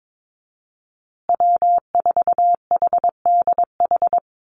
Morse code, heard as a keyed tone, about 22 wpm, 700 Hz. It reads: W4HDH
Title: USA Call Signs Repeated 2x 22wpm